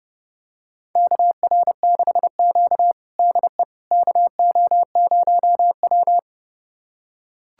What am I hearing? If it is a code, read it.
Morse code, heard as a keyed tone, about 32 wpm, 700 Hz. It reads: KR6Q DE KO0W
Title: USA Callsign DE Callsign - 4 Characters - 30wpm